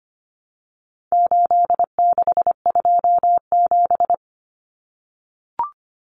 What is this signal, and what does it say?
Morse code, heard as a keyed tone, about 25 wpm, 700 Hz. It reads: 8627
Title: Numbers - 4-Digit 25wpm